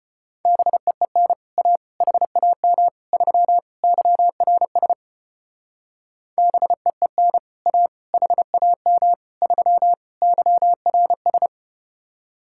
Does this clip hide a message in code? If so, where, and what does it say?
Yes: Morse code from 0.5 to 11.5 s, heard as a keyed tone, about 33 wpm, 700 Hz. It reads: BEEN A HAM 3 YRS BEEN A HAM 3 YRS